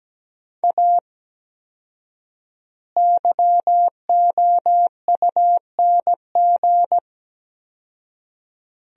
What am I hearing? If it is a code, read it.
Morse code, heard as a keyed tone, about 17 wpm, 700 Hz. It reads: A YOUNG
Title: 4x Word-Spacing - Sentences from Top 400 Words - 17wpm